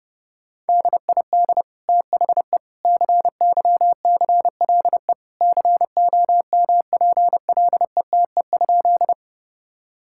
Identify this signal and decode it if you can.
Morse code, heard as a keyed tone, about 29 wpm, 700 Hz. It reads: DID THE CYCLE COMPLETE?
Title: Sentences from Top 1000 Words - Review Part 6 - 30wpm